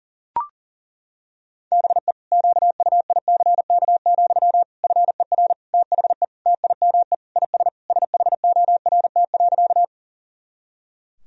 Morse code, heard as a keyed tone, about 40 wpm, 700 Hz. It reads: BE QUICK, FER THE TIME IS SHORT.
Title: Sentences from Top 500 Words and Abbreviations 40wpm